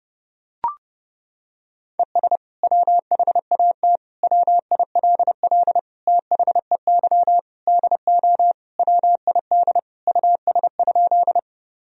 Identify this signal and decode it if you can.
Morse code, heard as a keyed tone, about 30 wpm, 700 Hz. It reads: ES WHAT WILL THEY DO WID US?
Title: Sentences from Top 200 Words and Abbreviations 30wpm